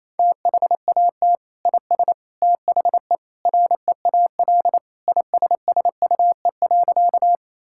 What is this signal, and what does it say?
Morse code, heard as a keyed tone, about 27 wpm, 700 Hz. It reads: THAT IS THE REAL ISSUE.
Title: Sentences from Top 1000 Words - Review Part 6 - 28wpm